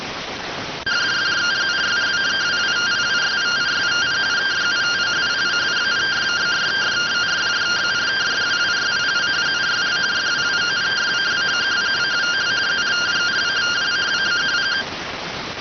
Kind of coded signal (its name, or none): two-tone teleprinter signal